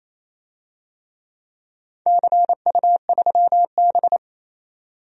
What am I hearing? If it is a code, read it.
Morse code, heard as a keyed tone, about 28 wpm, 700 Hz. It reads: CU3B
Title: DXCC-100 Call Signs - 4 Characters - 28wpm